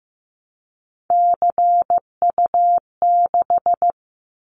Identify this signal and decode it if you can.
Morse code, heard as a keyed tone, about 15 wpm, 700 Hz. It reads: CU6
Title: Callsign Prefixes for Top 50 DXCC - 15wpm